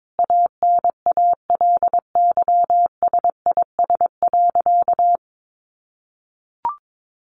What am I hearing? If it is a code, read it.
Morse code, heard as a keyed tone, about 22 wpm, 700 Hz. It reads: ANALYSIS.
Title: Sentences from Top 1400 Words 22wpm